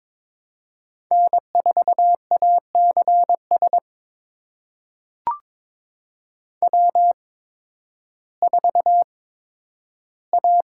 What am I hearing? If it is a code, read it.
Morse code, heard as a keyed tone, about 23 wpm, spaced as for 12 wpm, 700 Hz. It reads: N4ACS W4A
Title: USA Call Signs - 5 Characters - 8x Character Spacing - 22wpm